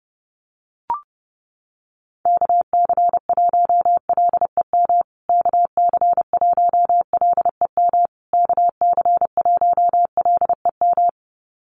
Morse code, heard as a keyed tone, about 30 wpm, 700 Hz. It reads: KC1LEM KC1LEM KC1LEM
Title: USA Call Signs Repeated 3x 30wpm